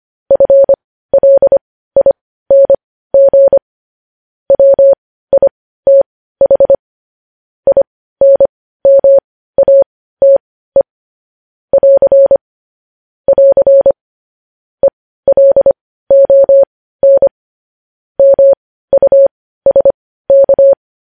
Morse code, heard as a keyed tone, about 25 wpm, spaced as for 15 wpm, 550 Hz. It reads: FLING WITH INMATE <AR> <AR> ELON MUSK
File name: podcast / News headlines-1x-25x15